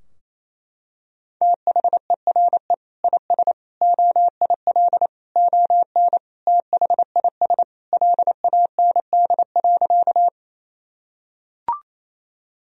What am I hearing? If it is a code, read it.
Morse code, heard as a keyed tone, about 28 wpm, 700 Hz. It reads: THERE IS OIL ON THIS LAND.
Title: Sentences from Top 1000 Words - Review Part 4 - 28wpm